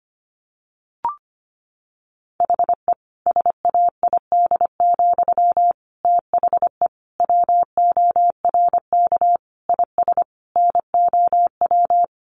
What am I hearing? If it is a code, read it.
Morse code, heard as a keyed tone, about 25 wpm, 700 Hz. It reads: HE SAID, THE WORK IS NOW
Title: Sentences from Top 400 Words 25wpm